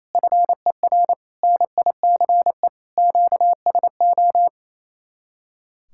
Morse code, heard as a keyed tone, about 34 wpm, 700 Hz. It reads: FER NICE QSO
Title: Speed-Racing Common QSO Phrases - 28wpm